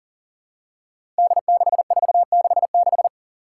Morse code, heard as a keyed tone, about 40 wpm, 700 Hz. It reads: D6466
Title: Five Challenging Letter-Number Combinations 40wpm